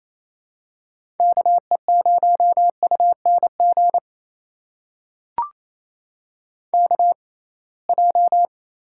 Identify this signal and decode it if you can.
Morse code, heard as a keyed tone, about 28 wpm, spaced as for 21 wpm, 700 Hz. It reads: KE0UNG KJ
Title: USA Call Signs - 6 Characters - 6x Character Spacing - 28wpm